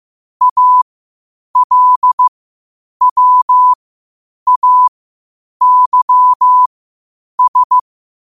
Morse code, heard as a keyed tone, about 15 wpm, spaced as for 8 wpm, 1000 Hz. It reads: ALWAYS
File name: qotdfiles / 8wpm